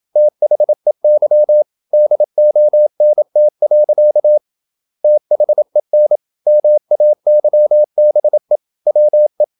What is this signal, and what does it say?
Morse code, heard as a keyed tone, about 27 wpm, 600 Hz. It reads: THEY DONT. THEN MAYBE WE
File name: Never 0027wpm